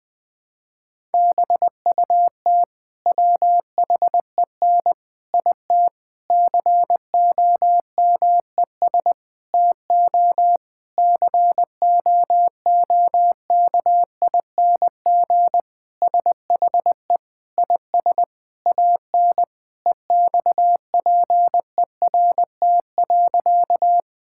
Morse code, heard as a keyed tone, about 20 wpm, 700 Hz. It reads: BUT WHEN IT COMES TO COOKING SHE IS AN EXPERT.
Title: Sentences with 5-Word Phrases - 20wpm